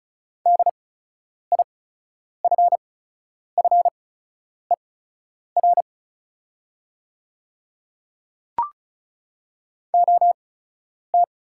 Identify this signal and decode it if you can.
Morse code, heard as a keyed tone, about 36 wpm, spaced as for 10 wpm, 700 Hz. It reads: DIFFER OT
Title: 8x Character Spacing - Top 200 Words - 35wpm